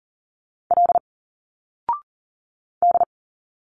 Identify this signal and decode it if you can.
Morse code, heard as a keyed tone, about 39 wpm, 700 Hz. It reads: L D
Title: Single Letters 40wpm